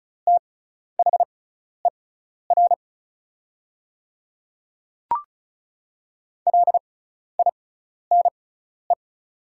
Morse code, heard as a keyed tone, about 36 wpm, spaced as for 12 wpm, 700 Hz. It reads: THER LINE
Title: 6x Character Spacing - Top 200 Words - 35wpm